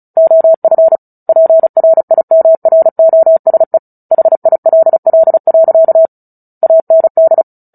Morse code, heard as a keyed tone, about 34 wpm, 650 Hz. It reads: OF PRIMROSE HILL. AND
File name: Book Two / 09-Chapter 9 - 35 WPM